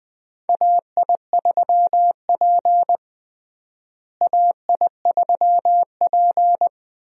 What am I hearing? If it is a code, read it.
Morse code, heard as a keyed tone, about 20 wpm, 700 Hz. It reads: AI3P AI3P
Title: USA Call Signs - 4 Characters - Repeated 3x - 20wpm